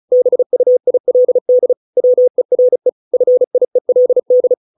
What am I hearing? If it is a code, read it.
Morse code, heard as a keyed tone, about 34 wpm, 500 Hz. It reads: BUILD WERE FIELD